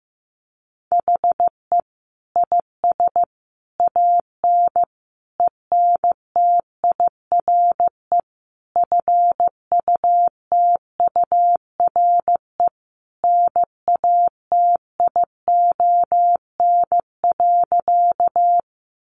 Morse code, heard as a keyed tone, about 15 wpm, 700 Hz. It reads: HE IS AN ENTIRE FUTURE NATION.
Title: A Sound Of Thunder 15wpm